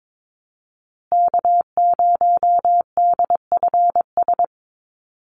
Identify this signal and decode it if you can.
Morse code, heard as a keyed tone, about 22 wpm, 700 Hz. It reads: K0DFS
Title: USA Call Signs Repeated 3x 22wpm